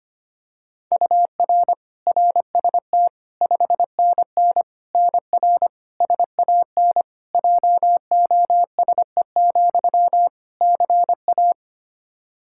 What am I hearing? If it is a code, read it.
Morse code, heard as a keyed tone, about 25 wpm, 700 Hz. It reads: UR RST 5NN NR SAN JOSE, CA
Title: Practice QSO - Exchange 1 - Beginning - 25wpm